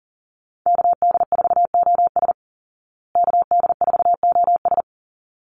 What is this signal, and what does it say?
Morse code, heard as a keyed tone, about 40 wpm, 700 Hz. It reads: KD4OS KD4OS
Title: Call Signs Repeated 3x - 40wpm